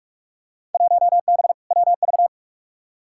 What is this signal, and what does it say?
Morse code, heard as a keyed tone, about 45 wpm, 700 Hz. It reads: 1B WV
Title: Field Day Exchanges - 45wpm